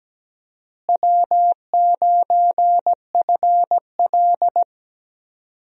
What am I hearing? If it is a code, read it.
Morse code, heard as a keyed tone, about 17 wpm, 700 Hz. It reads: W9FL
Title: USA Call Signs - 4 Characters - Repeated 2x - 17wpm